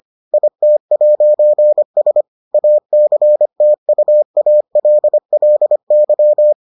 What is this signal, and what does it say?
Morse code, heard as a keyed tone, about 25 wpm, 600 Hz. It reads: IT'S ACTUALLY